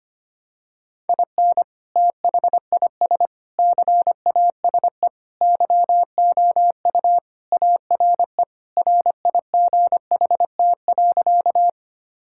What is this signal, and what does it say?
Morse code, heard as a keyed tone, about 25 wpm, 700 Hz. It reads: IN THIS CASE YOU ARE RIGHT.
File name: practice / 25wpm / Sentences with 3-Word Phrases - 25wpm